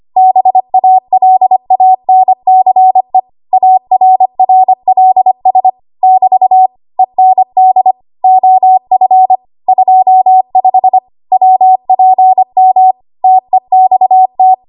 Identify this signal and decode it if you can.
Morse code, heard as a keyed tone, about 25 wpm, 750 Hz. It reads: BALANCE ARRLS <BT> END OF 25 WPM TEXT
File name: Archive / 25 WPM / 25WPM